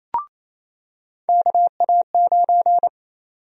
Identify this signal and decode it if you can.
Morse code, heard as a keyed tone, about 27 wpm, 700 Hz. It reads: KA9
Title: USA Call Sign Prefixes - 28wpm